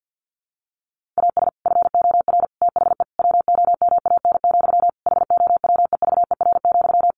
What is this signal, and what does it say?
Morse code, heard as a keyed tone, about 50 wpm, 700 Hz. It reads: AS FOR THE WOMAN, HOWEVER,
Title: Sentences from Top 1100 Words 50wpm